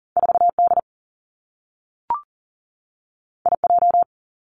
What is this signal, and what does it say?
Morse code, heard as a keyed tone, about 42 wpm, 700 Hz. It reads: 4D IJ
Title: Two Challenging Letter-Number Combinations 40wpm